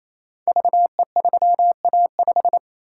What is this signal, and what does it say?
Morse code, heard as a keyed tone, about 27 wpm, 700 Hz. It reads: VE3A5
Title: Code Groups - 5 Letter:Numbers - 28wpm